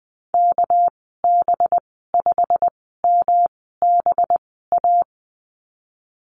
Morse code, heard as a keyed tone, about 20 wpm, spaced as for 15 wpm, 700 Hz. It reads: KB5MBA
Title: 207 Lesson 33g - Letter j - Call Signs 20wpm